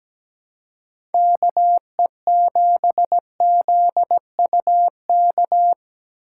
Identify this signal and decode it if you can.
Morse code, heard as a keyed tone, about 17 wpm, 700 Hz. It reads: KE7ZUK
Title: USA Call Signs - 6 Characters - 6x Character Spacing - 17wpm